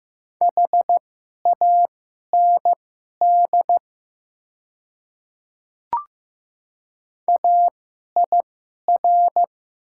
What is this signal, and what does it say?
Morse code, heard as a keyed tone, about 15 wpm, spaced as for 11 wpm, 700 Hz. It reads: HAND AIR
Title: 2x Character Spacing - Top 200 Words - 15wpm